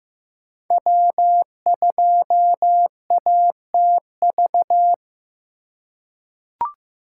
Morse code, heard as a keyed tone, about 15 wpm, 700 Hz. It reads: W2ATV
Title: USA Call Signs - 3x Character Spacing - 15wpm